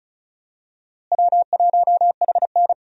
Morse code, heard as a keyed tone, about 34 wpm, 700 Hz. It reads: W1HN
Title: USA Call Signs - 4 Characters - Repeated 3x - 35wpm